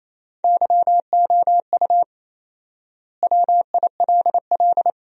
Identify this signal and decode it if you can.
Morse code, heard as a keyed tone, about 28 wpm, 700 Hz. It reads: YOU WILL
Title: 4x Word-Spacing - Sentences from Top 500 Words - 28wpm